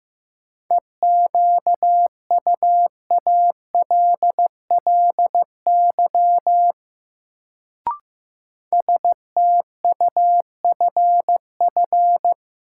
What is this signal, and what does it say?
Morse code, heard as a keyed tone, about 15 wpm, 700 Hz. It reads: EQUALLY STUFF
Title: Sentences from Top 1400 Words - New Word Drill Alt - 15wpm